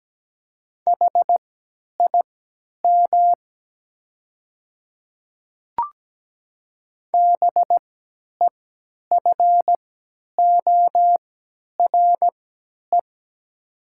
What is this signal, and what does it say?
Morse code, heard as a keyed tone, about 17 wpm, spaced as for 10 wpm, 700 Hz. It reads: HIM BEFORE
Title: 3x Character Spacing - Top 200 Words - 17wpm